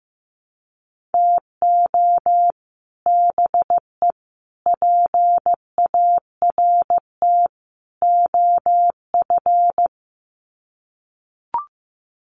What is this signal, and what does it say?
Morse code, heard as a keyed tone, about 15 wpm, 700 Hz. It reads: TO BE PART OF
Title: Sets of 4 Words 15wpm